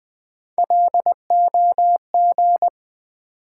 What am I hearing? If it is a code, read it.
Morse code, heard as a keyed tone, about 20 wpm, 700 Hz. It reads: LOG
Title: Sentences from Top 1400 Words - New Word Drill - 20wpm